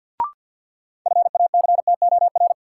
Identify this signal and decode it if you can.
Morse code, heard as a keyed tone, about 53 wpm, 700 Hz. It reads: FACTOR
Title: Sentences from Top 1300 Words - New Word Drill - 50wpm